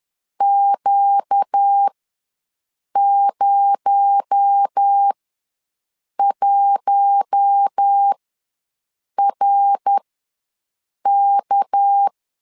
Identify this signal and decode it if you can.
Morse code, heard as a keyed tone, about 10 wpm, spaced as for 6 wpm, 800 Hz. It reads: Q01RK
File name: LEZIONI / 6 PKX0 / 6 PKX0 ws 10wpm 4220